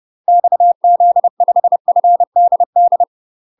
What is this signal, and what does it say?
Morse code, heard as a keyed tone, about 30 wpm, 700 Hz. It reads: XZ5FDD
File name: UK MASTER QSOs TRAINING / 30WPM / UK MASTER QSOs 0050